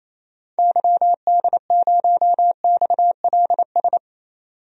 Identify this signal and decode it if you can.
Morse code, heard as a keyed tone, about 28 wpm, 700 Hz. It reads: YD0XLS
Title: DXCC-100 Call Signs - 6 Characters - 28wpm